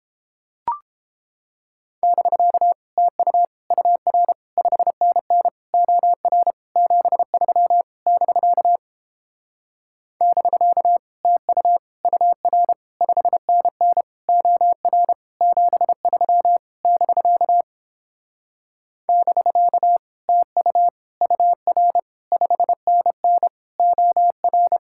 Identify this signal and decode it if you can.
Morse code, heard as a keyed tone, about 29 wpm, 700 Hz. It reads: <BK> TU UR 5NN OR 73 <BK> <BK> TU UR 5NN OR 73 <BK> <BK> TU UR 5NN OR